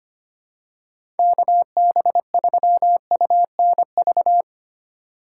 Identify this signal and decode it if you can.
Morse code, heard as a keyed tone, about 25 wpm, 700 Hz. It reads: KB3UNV
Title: USA Call Signs - 6 Characters - 3x Character Spacing - 25wpm